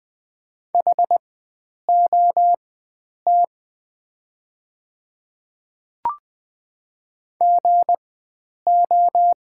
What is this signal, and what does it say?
Morse code, heard as a keyed tone, about 20 wpm, spaced as for 9 wpm, 700 Hz. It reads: HOT GO
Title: 4x Character Spacing - Top 100 Words - 20wpm